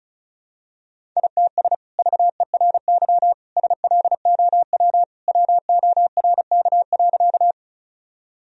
Morse code, heard as a keyed tone, about 36 wpm, 700 Hz. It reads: ITS VERY SLOW WORK.
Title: Sentences from Top 1000 Words - Review Part 2 - 35wpm